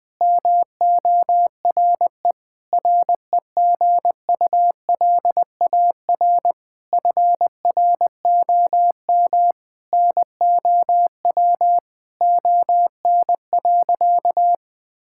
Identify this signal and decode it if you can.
Morse code, heard as a keyed tone, about 21 wpm, 700 Hz. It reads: MORE REGULAR FROM NOW ON.
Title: Sentences from Top 1000 Words - Review Part 6 - 20wpm